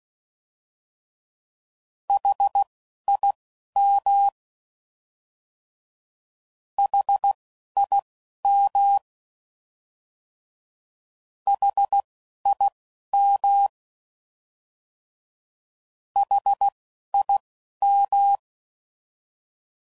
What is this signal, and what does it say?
Morse code, heard as a keyed tone, about 16 wpm, spaced as for 11 wpm, 800 Hz. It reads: HIM HIM HIM HIM